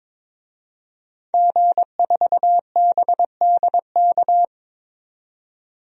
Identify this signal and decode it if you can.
Morse code, heard as a keyed tone, about 22 wpm, 700 Hz. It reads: G4BDK
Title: DXCC-100 Call Signs - 5 Characters - 22wpm